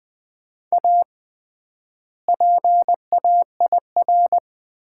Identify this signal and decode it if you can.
Morse code, heard as a keyed tone, about 20 wpm, 700 Hz. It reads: A PAIR
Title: 3x Word-Spacing - Sets of 3 Words - 20wpm